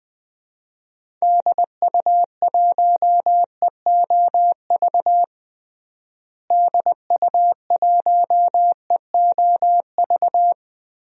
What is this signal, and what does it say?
Morse code, heard as a keyed tone, about 20 wpm, 700 Hz. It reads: DU1EOV DU1EOV
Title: DXCC-100 Call Signs - 6 Characters - Repeated 2x - 20wpm